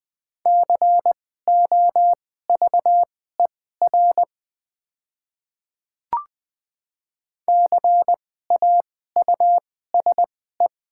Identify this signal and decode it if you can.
Morse code, heard as a keyed tone, about 20 wpm, spaced as for 14 wpm, 700 Hz. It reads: COVER CAUSE
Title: 2x Character Spacing - Top 300 Words - 20wpm